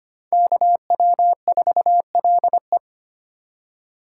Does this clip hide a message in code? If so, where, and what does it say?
Yes: Morse code from 0.3 to 2.8 s, heard as a keyed tone, about 33 wpm, 700 Hz. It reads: KW4LE